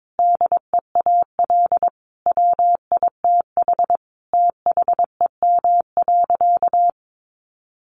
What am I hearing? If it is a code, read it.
Morse code, heard as a keyed tone, about 22 wpm, 700 Hz. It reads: DEAL WITH THEM.
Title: Sentences from Top 800 Words 22wpm